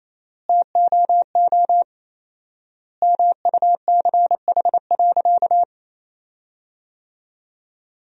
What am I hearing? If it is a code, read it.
Morse code, heard as a keyed tone, about 28 wpm, 700 Hz. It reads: TOO MUCH.
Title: 4x Word-Spacing - Sentences from Top 200 Words - 28wpm